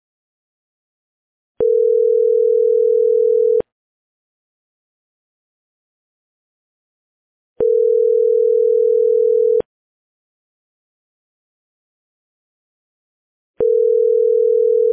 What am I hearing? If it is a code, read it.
Ringback tone (North America), 440 + 480 Hz.